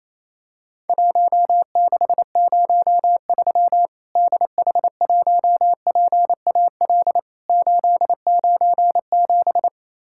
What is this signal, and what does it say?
Morse code, heard as a keyed tone, about 28 wpm, 700 Hz. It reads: 1603 DH1PAL 897
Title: Worked All Europe DX Contest - QTC Exchange - 28wpm